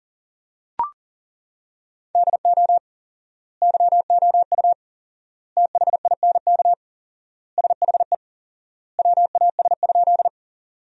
Morse code, heard as a keyed tone, about 40 wpm, 700 Hz. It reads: DO YOU THINK SHE WAS?
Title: 4x Word-Spacing - Sentences from Top 200 Words - 40wpm